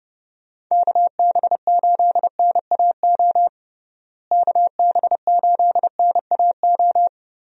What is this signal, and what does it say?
Morse code, heard as a keyed tone, about 30 wpm, 700 Hz. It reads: KB8NAO KB8NAO
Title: USA Call Signs - 6 Characters - Repeated 2x - 30wpm